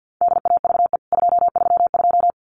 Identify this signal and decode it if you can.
Morse code, heard as a keyed tone, about 50 wpm, 700 Hz. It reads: DAVE 232
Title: CWT Exchange 50wpm